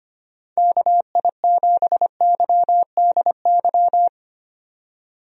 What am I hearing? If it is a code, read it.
Morse code, heard as a keyed tone, about 25 wpm, 700 Hz. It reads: KI7YDY
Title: USA Call Signs - 6 Characters - 25wpm